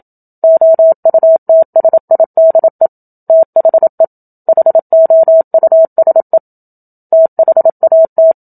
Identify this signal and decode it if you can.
Morse code, heard as a keyed tone, about 27 wpm, 650 Hz. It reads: OUTSIDE THE HOUSE THAT
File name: Book Two / 01-Chapter 1 - 27 WPM